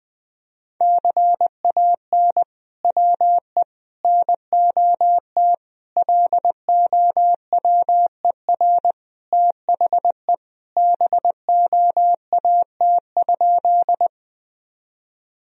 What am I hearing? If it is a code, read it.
Morse code, heard as a keyed tone, about 21 wpm, 700 Hz. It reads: CAN WE NOT LOWER THE BOAT?
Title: Sentences from Top 1000 Words - Review Part 4 - 20wpm